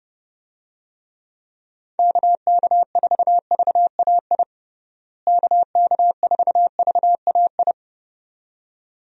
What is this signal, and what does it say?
Morse code, heard as a keyed tone, about 32 wpm, 700 Hz. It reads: KK4VAI KK4VAI
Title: USA Call Signs - 6 Characters - Repeated 2x - 30wpm